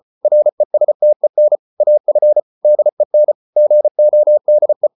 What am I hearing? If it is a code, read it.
Morse code, heard as a keyed tone, about 34 wpm, 600 Hz. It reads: RESTEN AF DEN GODE
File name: Ferie QRQ 34wpm